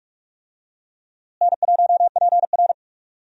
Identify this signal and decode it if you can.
Morse code, heard as a keyed tone, about 43 wpm, 700 Hz. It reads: N1PR
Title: USA Call Signs - 4 Characters - Repeated 2x - 45wpm